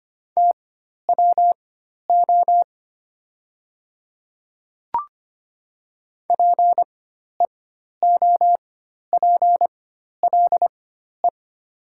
Morse code, heard as a keyed tone, about 25 wpm, spaced as for 12 wpm, 700 Hz. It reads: TWO PEOPLE
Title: 4x Character Spacing - Top 100 Words - 25wpm